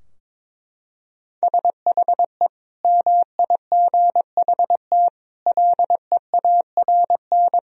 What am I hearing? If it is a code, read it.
Morse code, heard as a keyed tone, about 22 wpm, 700 Hz. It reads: SHE MIGHT LEARN
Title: Sentences from Top 1000 Words - Review Part 4 - 22wpm